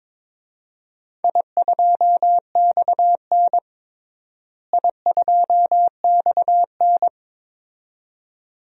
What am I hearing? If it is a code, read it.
Morse code, heard as a keyed tone, about 22 wpm, 700 Hz. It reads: I2XN I2XN
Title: DXCC-100 Call Signs - 4 Characters - Repeated 2x - 22wpm